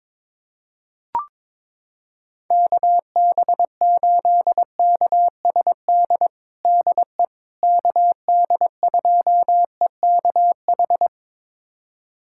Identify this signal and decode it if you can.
Morse code, heard as a keyed tone, about 22 wpm, 700 Hz. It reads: KB8KSD DE KD2EKH
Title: USA Callsign DE Callsign 22wpm